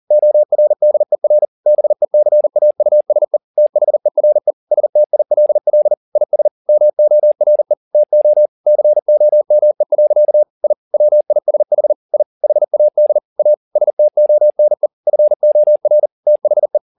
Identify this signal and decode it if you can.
Morse code, heard as a keyed tone, about 40 wpm, 600 Hz. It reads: ORDER BECAUSE THERE STILL IS MORE TO COME. I WISH I HAD A STONE FOR THE
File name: the old man and the sea 2 40wpm